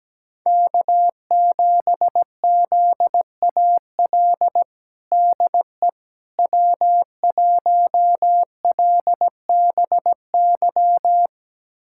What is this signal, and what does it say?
Morse code, heard as a keyed tone, about 17 wpm, 700 Hz. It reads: K7ZAL DE W1LBY
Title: USA Callsign DE Callsign - 5 Characters - 17wpm